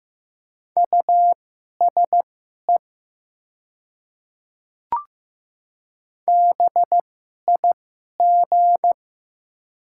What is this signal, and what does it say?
Morse code, heard as a keyed tone, about 15 wpm, spaced as for 11 wpm, 700 Hz. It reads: USE BIG
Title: 2x Character Spacing - Top 200 Words - 15wpm